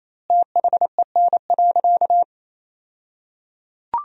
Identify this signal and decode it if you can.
Morse code, heard as a keyed tone, about 28 wpm, 700 Hz. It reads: THEN.
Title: 2x Word-Spacing - Sentences from Top 200 Words - 28wpm